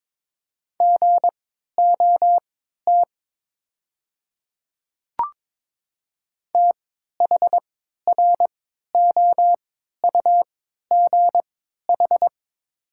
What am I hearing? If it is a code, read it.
Morse code, heard as a keyed tone, about 22 wpm, spaced as for 12 wpm, 690 Hz. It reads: GOT THROUGH
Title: 3x Character Spacing - Top 300 Words - 22wpm